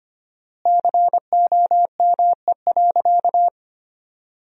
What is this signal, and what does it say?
Morse code, heard as a keyed tone, about 25 wpm, 700 Hz. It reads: COME.
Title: 4x Word-Spacing - Sentences from Top 100 Words - 25wpm